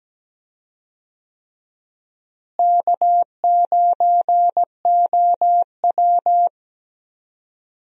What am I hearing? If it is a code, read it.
Morse code, heard as a keyed tone, about 17 wpm, 700 Hz. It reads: K9OW
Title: USA Call Signs - 4 Characters - Repeated 3x - 17wpm